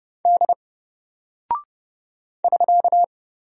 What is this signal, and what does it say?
Morse code, heard as a keyed tone, about 30 wpm, 700 Hz. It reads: D <SK>
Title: Single Character (Letter:Number:Punctuation:Prosign) Alt 30wpm